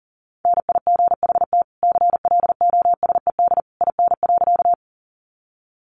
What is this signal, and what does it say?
Morse code, heard as a keyed tone, about 37 wpm, 700 Hz. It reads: NIGHT CLOSED IN.
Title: Sentences from Top 1400 Words 40wpm